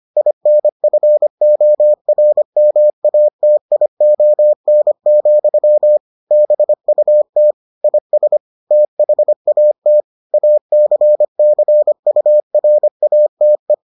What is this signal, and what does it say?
Morse code, heard as a keyed tone, about 25 wpm, 600 Hz. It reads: INFORMATION, BUT IS THAT ACCURATE